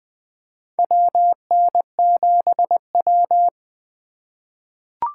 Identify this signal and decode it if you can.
Morse code, heard as a keyed tone, about 20 wpm, 700 Hz. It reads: WN7W
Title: USA Call Signs - 4 Characters - 3x Character Spacing - 20wpm